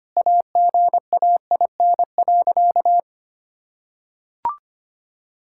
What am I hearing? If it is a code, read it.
Morse code, heard as a keyed tone, about 25 wpm, 700 Hz. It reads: AGAIN.
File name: practice / 25wpm / Sentences with 4-Word Phrases - 25wpm